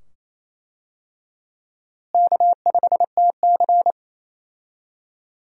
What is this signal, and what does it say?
Morse code, heard as a keyed tone, about 28 wpm, 700 Hz. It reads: K5TC
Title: USA Call Signs - 4 Characters - Repeated 2x - 28wpm